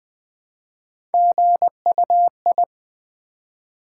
Morse code, heard as a keyed tone, about 20 wpm, 700 Hz. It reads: GUI
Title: Build A Word - 1 Letter At A Time - 3 to 5-Letter Words - 20wpm